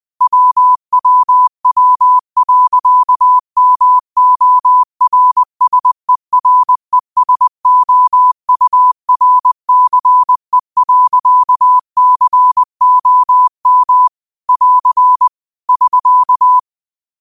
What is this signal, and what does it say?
Morse code, heard as a keyed tone, about 20 wpm, 1000 Hz. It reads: WWW.MORSERESOURCE.COM <AR> <SK>
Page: Quotes for Sat, 16 Aug 2025 in Morse Code at 20 words per minute.